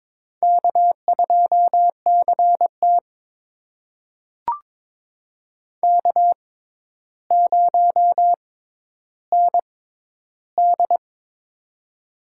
Morse code, heard as a keyed tone, about 22 wpm, spaced as for 11 wpm, 700 Hz. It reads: K2CT K0ND
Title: USA Call Signs - 4 Characters - 6x Character Spacing - 22wpm